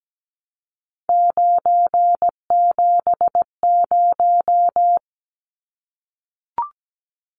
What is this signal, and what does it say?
Morse code, heard as a keyed tone, about 17 wpm, 700 Hz. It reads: 970
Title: Numbers - 3-Digit 17wpm